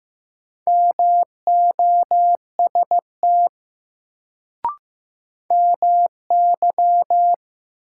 Morse code, heard as a keyed tone, about 15 wpm, 700 Hz. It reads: MOST MY
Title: Sentences from Top 200 Words - New Word Drill - 15wpm